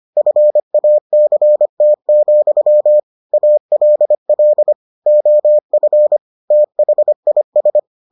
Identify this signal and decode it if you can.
Morse code, heard as a keyed tone, about 25 wpm, 600 Hz. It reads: FACT, ALL OF THIS